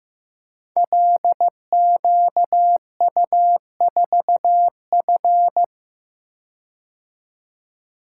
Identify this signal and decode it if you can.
Morse code, heard as a keyed tone, about 15 wpm, 700 Hz. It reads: LQU4F
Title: Code Groups - 5 Letter:Numbers - 15wpm